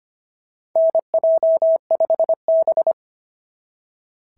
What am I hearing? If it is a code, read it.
Morse code, heard as a keyed tone, about 25 wpm, 690 Hz. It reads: NJ5B
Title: SOTA QSO Practice 25wpm